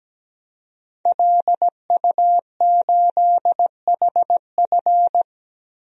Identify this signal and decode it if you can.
Morse code, heard as a keyed tone, about 17 wpm, 700 Hz. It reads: LU8HF
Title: DXCC-100 Call Signs - 5 Characters - 17wpm